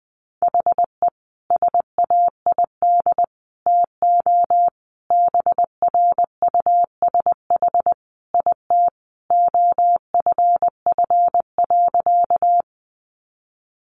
Morse code, heard as a keyed tone, about 19 wpm, 700 Hz. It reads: HE SAID TO BRUSH IT OFF.
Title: Sentences from Top 1100 Words 20wpm